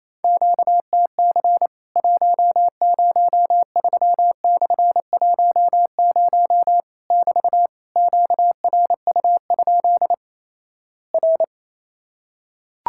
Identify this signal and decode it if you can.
Morse code, heard as a keyed tone, about 28 wpm, 700 Hz. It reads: QTC 103/10 <BT> QRU? R
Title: Worked All Europe DX Contest - QTC Exchange - 28wpm